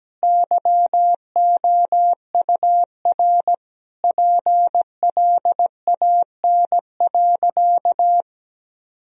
Morse code, heard as a keyed tone, about 17 wpm, 700 Hz. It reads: YOUR PLAN.
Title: Sentences from Top 1000 Words - Review Part 2 - 17wpm